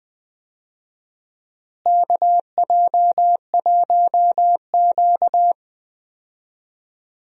Morse code, heard as a keyed tone, about 20 wpm, 700 Hz. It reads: KJ1Q
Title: USA Call Signs - 4 Characters - Repeated 2x - 20wpm